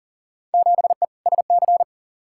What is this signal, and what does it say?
Morse code, heard as a keyed tone, about 40 wpm, 700 Hz. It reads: 7E SC
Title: Field Day Exchanges Encore - 40wpm